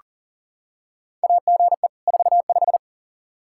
Morse code, heard as a keyed tone, about 40 wpm, 700 Hz. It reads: AGE 45
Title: Practice QSO - Exchange 3 - More Info - 40wpm